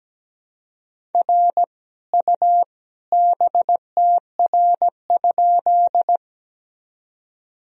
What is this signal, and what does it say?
Morse code, heard as a keyed tone, about 17 wpm, 700 Hz. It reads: R U BTR?
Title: Sentences from Top 400 Words and Abbreviations 17wpm